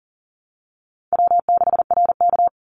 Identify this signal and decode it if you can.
Morse code, heard as a keyed tone, about 40 wpm, 700 Hz. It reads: W6RK
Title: Easy Call Signs Repeated 3x - 40wpm